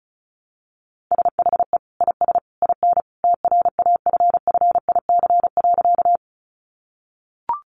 Morse code, heard as a keyed tone, about 34 wpm, 700 Hz. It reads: SHE IS IN TRAFFIC.
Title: Sentences from Top 1300 Words 35wpm